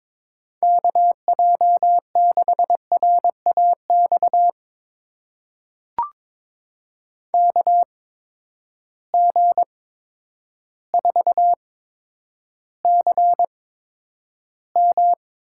Morse code, heard as a keyed tone, about 22 wpm, spaced as for 10 wpm, 700 Hz. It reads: KJ6RAX KG4CM
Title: USA Call Signs - 6 Characters - 8x Character Spacing - 22wpm